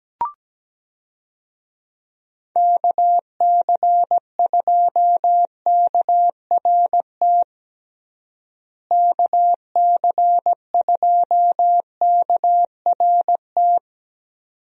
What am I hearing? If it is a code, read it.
Morse code, heard as a keyed tone, about 17 wpm, 700 Hz. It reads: KC2KRT KC2KRT
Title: USA Call Signs - 6 Characters - Repeated 2x - 17wpm